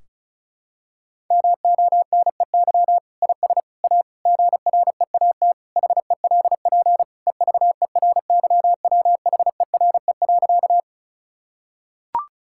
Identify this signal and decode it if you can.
Morse code, heard as a keyed tone, about 34 wpm, 700 Hz. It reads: MONEY IS A GREAT HELP EVERYWHERE.
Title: Sentences from Top 1600 Words 35wpm